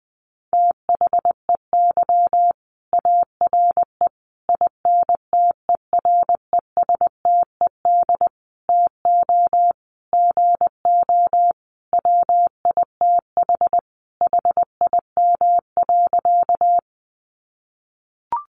Morse code, heard as a keyed tone, about 20 wpm, 700 Hz. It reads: THEY ARE INTERESTED TO GO WITH HIM.
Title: Sentences from Top 900 Words 20wpm